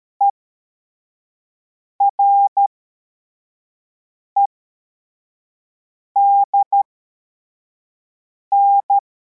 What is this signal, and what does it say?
Morse code, heard as a keyed tone, about 13 wpm, spaced as for 4 wpm, 800 Hz. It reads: EREDN
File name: formation / cw / Text23